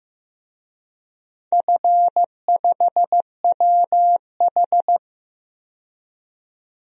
Morse code, heard as a keyed tone, about 15 wpm, 700 Hz. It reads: F5WH
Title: DXCC-100 Call Signs - 4 Characters - 15wpm